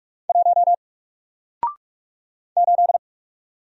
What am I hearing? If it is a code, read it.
Morse code, heard as a keyed tone, about 44 wpm, 700 Hz. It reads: 1 8
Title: Single Letter-Number Alt 45wpm